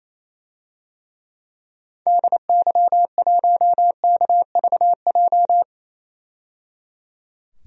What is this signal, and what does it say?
Morse code, heard as a keyed tone, about 28 wpm, 700 Hz. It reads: DY1KVJ
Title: DXCC-100 Call Signs - 6 Characters - 28wpm